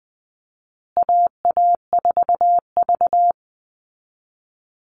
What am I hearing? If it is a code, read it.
Morse code, heard as a keyed tone, about 19 wpm, 700 Hz. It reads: AA4V
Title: Call Signs Repeated 2x - 20wpm